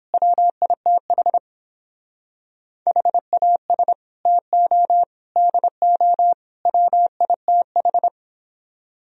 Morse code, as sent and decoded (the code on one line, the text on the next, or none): .-- .. - .... / .... .- ... / - --- / -.. --- / .-- .. - ....
WITH HAS TO DO WITH